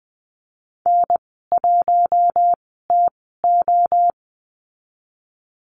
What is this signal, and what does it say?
Morse code, heard as a keyed tone, about 20 wpm, spaced as for 15 wpm, 700 Hz. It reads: N1TO
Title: 048 Lesson 9g - Number 4 - Call Signs 20wpm